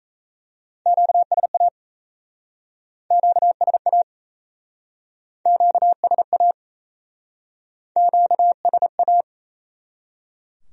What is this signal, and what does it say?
Morse code, heard as a keyed tone, about 37 wpm, 700 Hz. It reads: QSA QSA QSA QSA